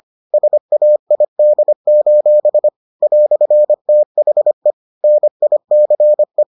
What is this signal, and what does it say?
Morse code, heard as a keyed tone, about 25 wpm, 600 Hz. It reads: SAID: "THE NICE